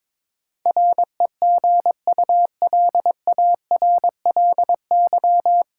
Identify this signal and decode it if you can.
Morse code, heard as a keyed tone, about 22 wpm, 700 Hz. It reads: REGULARLY
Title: Sentences from Top 1400 Words - New Word Drill - 22wpm